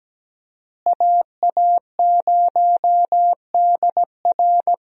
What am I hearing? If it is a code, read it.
Morse code, heard as a keyed tone, about 17 wpm, 700 Hz. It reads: AA0DR
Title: USA Call Signs - 5 Characters - Repeated 2x - 17wpm